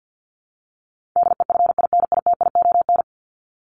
Morse code, heard as a keyed tone, about 50 wpm, 700 Hz. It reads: DEFINITION
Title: Top 501-750 Words 50wpm